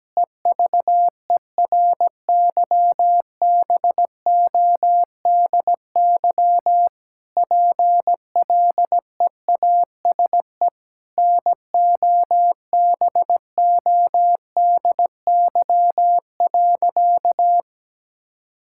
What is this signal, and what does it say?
Morse code, heard as a keyed tone, about 17 wpm, 700 Hz. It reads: EVERYBODY PLEASE NOBODY.
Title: Sentences from Top 1600 Words 17wpm